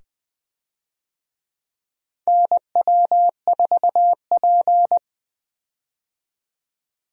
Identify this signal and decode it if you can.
Morse code, heard as a keyed tone, about 21 wpm, 700 Hz. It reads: NW4P
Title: USA Call Signs - 4 Characters - 20wpm